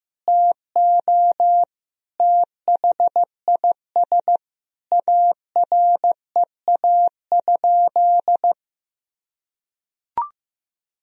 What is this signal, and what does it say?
Morse code, heard as a keyed tone, about 15 wpm, 700 Hz. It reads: TO THIS AREA?
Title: Sentences from Top 1000 Words - Review Part 3 - 15wpm